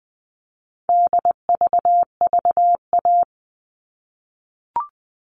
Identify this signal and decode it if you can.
Morse code, heard as a keyed tone, about 20 wpm, 700 Hz. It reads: DVVA
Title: All Cut Numbers - 4-Digit 20wpm